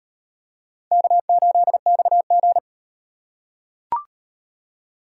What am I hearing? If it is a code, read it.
Morse code, heard as a keyed tone, about 36 wpm, 700 Hz. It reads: K8XG